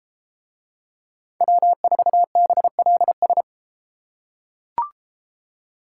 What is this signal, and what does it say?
Morse code, heard as a keyed tone, about 32 wpm, 700 Hz. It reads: W4BLS